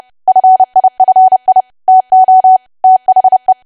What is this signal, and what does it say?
Morse code, heard as a keyed tone, about 30 wpm, 750 Hz. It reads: FIFI TO THE
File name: cwa / W1AW-Level3 / 30WPM